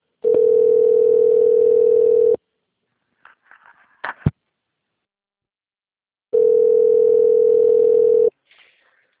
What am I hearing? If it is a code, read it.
Ringback tone (North America), 440 + 480 Hz.
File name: rg-1600-unknown- - - .52312